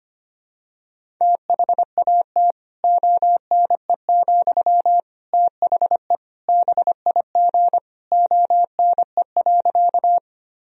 Morse code, heard as a keyed tone, about 25 wpm, 700 Hz. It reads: THAT ONE, THE BIG ONE.
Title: Sentences from Top 1000 Words - Review Part 1 - 25wpm